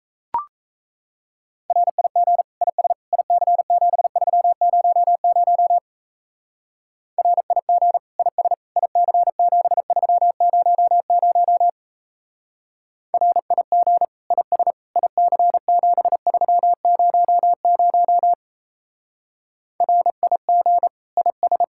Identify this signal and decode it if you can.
Morse code, heard as a keyed tone, about 35 wpm, 700 Hz. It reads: RIG IS IC7300 RIG IS IC7300 RIG IS IC7300 RIG IS